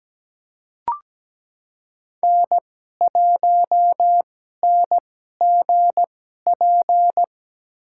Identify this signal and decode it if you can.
Morse code, heard as a keyed tone, about 17 wpm, spaced as for 12 wpm, 700 Hz. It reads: N1NGP
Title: USA Call Signs - 2x Character Spacing - 17wpm